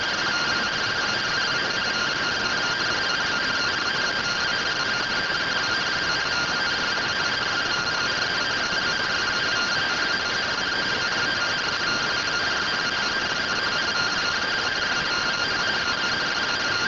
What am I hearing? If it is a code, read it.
Two-tone teleprinter signal, about 170 Hz shift.